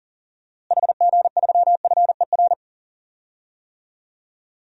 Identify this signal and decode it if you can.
Morse code, heard as a keyed tone, about 40 wpm, 700 Hz. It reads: HG3FER
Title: DXCC-100 Call Signs - 6 Characters - Repeated 2x - 40wpm